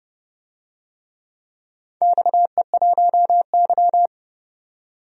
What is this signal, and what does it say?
Morse code, heard as a keyed tone, about 30 wpm, 700 Hz. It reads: XE1Y
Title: DXCC-100 Call Signs - 4 Characters - 30wpm